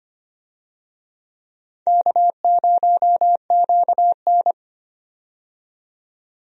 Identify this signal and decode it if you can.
Morse code, heard as a keyed tone, about 25 wpm, 700 Hz. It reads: K0QN
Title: USA Call Signs - 4 Characters - 25wpm